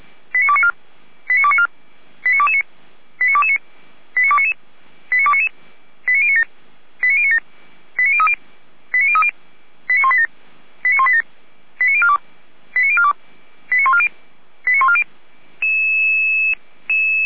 Five-tone selective call (ZVEI), heard as quick runs of five tones, tones of about 70 ms. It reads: 89294 89294 89229 89229 89299 89299 89097 89097 89030 89030 89187 89187 89042 89042 89140 89140